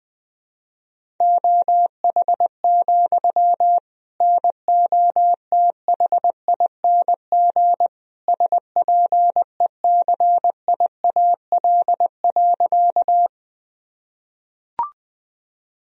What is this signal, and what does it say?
Morse code, heard as a keyed tone, about 21 wpm, 700 Hz. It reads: OH, NOTHING SPECIAL.
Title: Sentences from Top 1000 Words - Review Part 3 - 20wpm